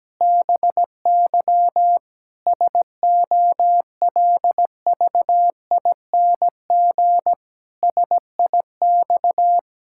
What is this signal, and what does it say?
Morse code, heard as a keyed tone, about 17 wpm, 700 Hz. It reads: BY SOLVING SIX
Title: Pangram Challenge v2 - 17wpm